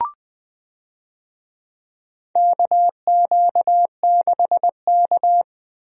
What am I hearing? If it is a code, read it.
Morse code, heard as a keyed tone, about 21 wpm, 700 Hz. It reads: KQ6K
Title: USA Call Signs - 4 Characters - Repeated 2x - 20wpm